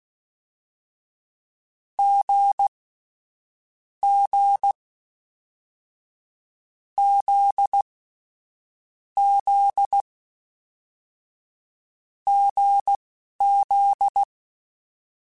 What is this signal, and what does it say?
Morse code, heard as a keyed tone, about 16 wpm, spaced as for 11 wpm, 800 Hz. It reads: G G Z Z GZ
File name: Short Runs / Short Runs-mp3s / awj1ndb6gz7 @10